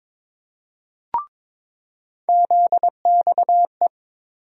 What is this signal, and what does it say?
Morse code, heard as a keyed tone, about 22 wpm, 700 Hz. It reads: ZXE
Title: USA Call Sign Postfixes - 3 Letters 22wpm